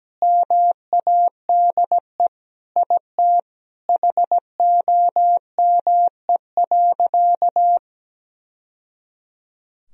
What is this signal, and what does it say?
Morse code, heard as a keyed tone, about 17 wpm, 700 Hz. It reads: MADE IT HOME.
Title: Sentences from Top 1000 Words - Review Part 2 - 17wpm